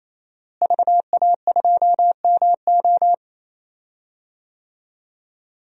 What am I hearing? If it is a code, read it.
Morse code, heard as a keyed tone, about 28 wpm, 700 Hz. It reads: VA2MO
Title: DXCC-100 Call Signs - 5 Characters - Repeated 3x - 28wpm